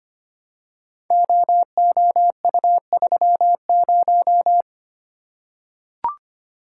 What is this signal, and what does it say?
Morse code, heard as a keyed tone, about 25 wpm, 700 Hz. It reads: OOU30
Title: Code Groups - 5 Letter:Numbers - 25wpm